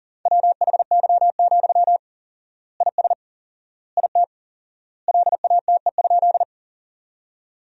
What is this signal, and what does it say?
Morse code, heard as a keyed tone, about 40 wpm, 700 Hz. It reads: WHY, IS IT LATE?
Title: 4x Word-Spacing - Sentences from Top 300 Words - 40wpm